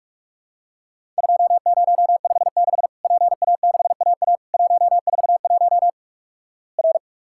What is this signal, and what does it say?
Morse code, heard as a keyed tone, about 46 wpm, 700 Hz. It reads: 2056 PA6AA 141 R
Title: Worked All Europe DX Contest - QTC Exchange - 45wpm